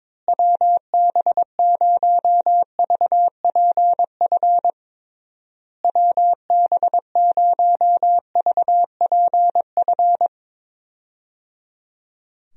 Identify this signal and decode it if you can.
Morse code, heard as a keyed tone, about 22 wpm, 700 Hz. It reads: WB0VPF WB0VPF
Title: USA Call Signs - 6 Characters - Repeated 3x - 22wpm